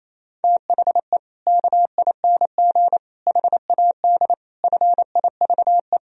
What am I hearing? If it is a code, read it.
Morse code, heard as a keyed tone, about 28 wpm, 700 Hz. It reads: THE KING HAD FIVE
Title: Sentences from Top 1000 Words - Review Part 2 - 28wpm